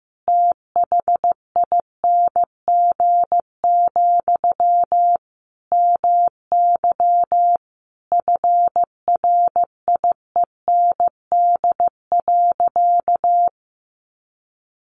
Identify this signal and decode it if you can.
Morse code, heard as a keyed tone, about 15 wpm, 700 Hz. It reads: THING, MY FRIEND.
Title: Sentences from Top 1200 Words 15wpm